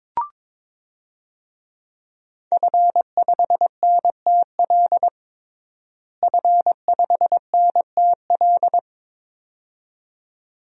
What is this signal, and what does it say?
Morse code, heard as a keyed tone, about 22 wpm, 700 Hz. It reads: F5NTL F5NTL
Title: DXCC-100 Call Signs - 5 Characters - Repeated 2x - 22wpm